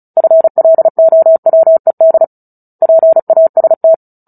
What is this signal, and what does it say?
Morse code, heard as a keyed tone, about 35 wpm, 650 Hz. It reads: FLOWED PAST
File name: Book Two / 09-Chapter 9 - 35 WPM